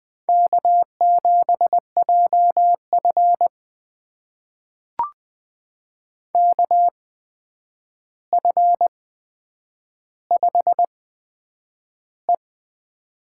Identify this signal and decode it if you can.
Morse code, heard as a keyed tone, about 20 wpm, spaced as for 9 wpm, 700 Hz. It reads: K7JF KF5E
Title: USA Call Signs - 4 Characters - 8x Character Spacing - 20wpm